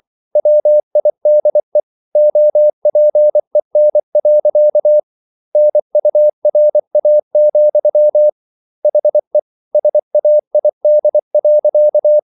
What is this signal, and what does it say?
Morse code, heard as a keyed tone, about 24 wpm, 600 Hz. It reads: WIDE OPEN. NURA, HE SAID.